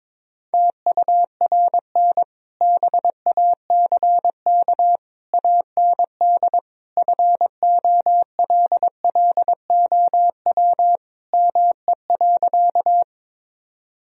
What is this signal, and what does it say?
Morse code, heard as a keyed tone, about 22 wpm, 700 Hz. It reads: TURN BACK AND FOLLOW ME.
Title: Sentences from Top 1000 Words - Review Part 1 - 22wpm